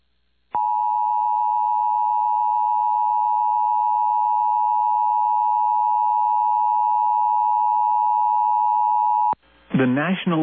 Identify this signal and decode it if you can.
Emergency Alert System (SAME) tones (the 853 and 960 Hz attention tone). It is heard from 0.5 to 9.4 s.
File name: eas / KUSO / KUSO